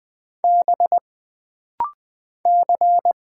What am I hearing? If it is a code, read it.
Morse code, heard as a keyed tone, about 20 wpm, 700 Hz. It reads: B C
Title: Single Character (Letter:Number:Punctuation:Prosign) - Rapid-Fire - 20wpm